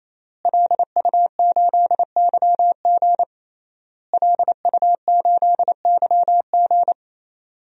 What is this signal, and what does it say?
Morse code, heard as a keyed tone, about 28 wpm, 700 Hz. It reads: LU8YG LU8YG
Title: DXCC-100 Call Signs - 5 Characters - Repeated 3x - 28wpm